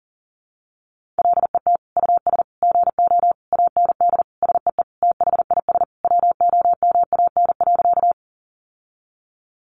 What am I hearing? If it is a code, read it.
Morse code, heard as a keyed tone, about 40 wpm, 700 Hz. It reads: LET US GO AND SEE THIS WOMAN.
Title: Sentences from Top 1100 Words 40wpm